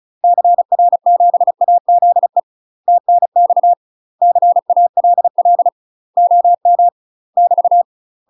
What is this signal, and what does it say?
Morse code, heard as a keyed tone, about 36 wpm, 700 Hz. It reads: CR7AZE TNX CALL OM <BT>
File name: UK MASTER QSOs TRAINING / 35WPM / UK MASTER QSOs 0016